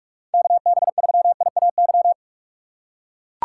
Morse code, heard as a keyed tone, about 43 wpm, 700 Hz. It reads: KB3IAY
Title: USA Call Signs - 6 Characters - 2x Character Spacing - 45wpm